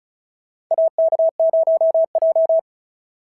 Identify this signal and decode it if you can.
Morse code, heard as a keyed tone, about 34 wpm, 690 Hz. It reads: AK0J
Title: SOTA QSO Practice 35wpm